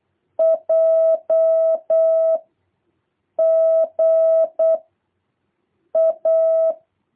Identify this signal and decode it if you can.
Morse code, heard as a keyed tone, about 8 wpm, 660 Hz. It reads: J G A
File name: jh8cbh / 2023cw